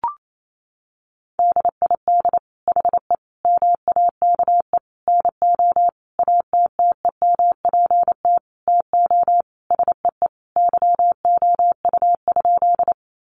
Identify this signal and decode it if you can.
Morse code, heard as a keyed tone, about 28 wpm, 700 Hz. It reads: DID HE MAKE NO ATTEMPT TO SEE YOU?
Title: Sentences from Top 1000 Words 28wpm